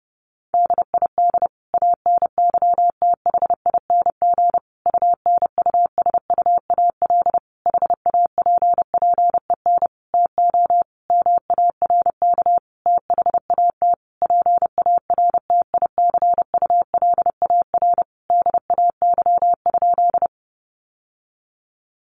Morse code, heard as a keyed tone, about 30 wpm, 700 Hz. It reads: DID ANYTHING UNUSUAL HAPPEN TO MARK THAT PARTICULAR DAY?
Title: Sentences from Top 1400 Words 30wpm